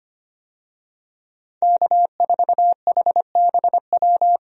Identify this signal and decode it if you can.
Morse code, heard as a keyed tone, about 25 wpm, 700 Hz. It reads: K4HBW
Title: USA Call Signs - 5 Characters - 25wpm